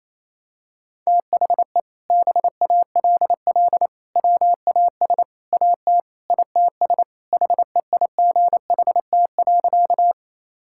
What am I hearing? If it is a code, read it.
Morse code, heard as a keyed tone, about 28 wpm, 700 Hz. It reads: THE BALL WAS AT ITS HEIGHT.
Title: Sentences from Top 1600 Words 28wpm